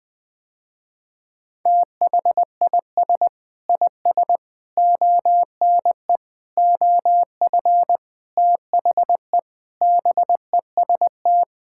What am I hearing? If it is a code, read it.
Morse code, heard as a keyed tone, about 20 wpm, 690 Hz. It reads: THIS IS ONE OF THE BEST
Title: Sentences with 4-Word Phrases - Spoken Beginning - 20wpm